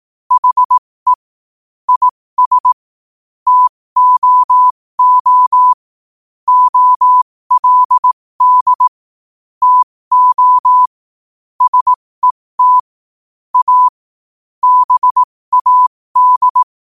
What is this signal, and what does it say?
Morse code, heard as a keyed tone, about 18 wpm, spaced as for 15 wpm, 1000 Hz. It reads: HE IS TOO OLD TO SET A BAD
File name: qotdfiles / 15wpm